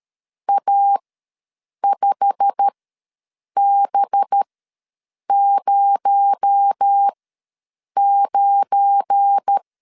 Morse code, heard as a keyed tone, about 13 wpm, spaced as for 7 wpm, 800 Hz. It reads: A5B09
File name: LEZIONI / 9 28 ws 12wpm 3354